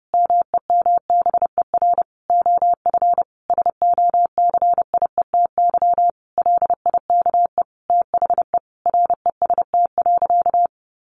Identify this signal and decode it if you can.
Morse code, heard as a keyed tone, about 30 wpm, 700 Hz. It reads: MEMBER OF SOCIETY LIKE THE REST.
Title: Sentences from Top 1100 Words 30wpm